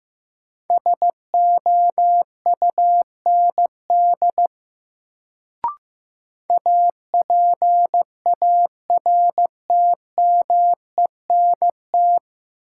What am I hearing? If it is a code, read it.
Morse code, heard as a keyed tone, about 15 wpm, 700 Hz. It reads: SOUND APARTMENT
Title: Sentences from Top 900 Words - New Word Drill - 15wpm